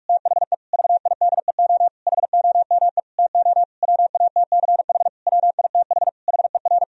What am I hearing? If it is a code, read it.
Morse code, heard as a keyed tone, about 41 wpm, 700 Hz. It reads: THE VIDEO HOME TO WATCH WITH HER
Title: Sentences from Top 1000 Words - Review Part 4 - 45wpm